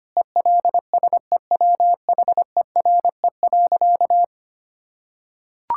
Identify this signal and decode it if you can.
Morse code, heard as a keyed tone, about 25 wpm, 700 Hz. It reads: ELSEWHERE.
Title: Sentences from Top 1600 Words 25wpm